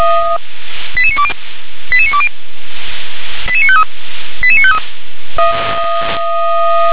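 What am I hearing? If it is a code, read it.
Five-tone selective call (ZVEI), heard as quick runs of five tones, tones of about 70 ms. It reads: 88920 88920 88953 88953
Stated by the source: ZVEI five-tone call received on SDR.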